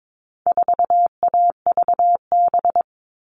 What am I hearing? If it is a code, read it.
Morse code, heard as a keyed tone, about 22 wpm, 700 Hz. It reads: 4AVB
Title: All Cut Numbers - 4-Digit 22wpm